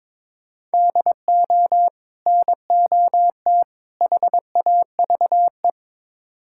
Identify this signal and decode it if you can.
Morse code, heard as a keyed tone, about 22 wpm, 700 Hz. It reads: DO NOT HAVE
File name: practice / 22wpm / Build A Phrase - 5-Word Phrases - 22wpm